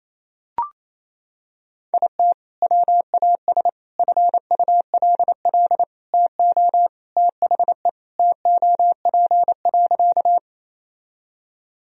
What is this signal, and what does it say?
Morse code, heard as a keyed tone, about 28 wpm, 700 Hz. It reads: IT WAS FULL TO THE TOP.
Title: Sentences from Top 1000 Words - Review Part 3 - 28wpm